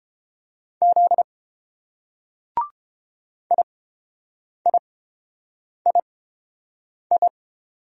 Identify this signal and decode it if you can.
Morse code, heard as a keyed tone, about 28 wpm, spaced as for 7 wpm, 700 Hz. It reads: Z IIII